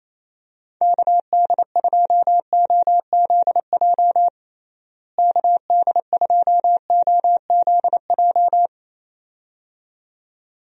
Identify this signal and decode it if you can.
Morse code, heard as a keyed tone, about 28 wpm, 700 Hz. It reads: KD2OZJ KD2OZJ
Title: USA Call Signs - 6 Characters - Repeated 2x - 28wpm